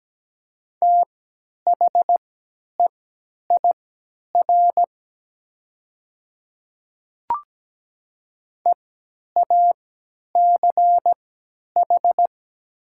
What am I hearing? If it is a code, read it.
Morse code, heard as a keyed tone, about 17 wpm, spaced as for 10 wpm, 700 Hz. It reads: THEIR EACH
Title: 3x Character Spacing - Top 100 Words - 17wpm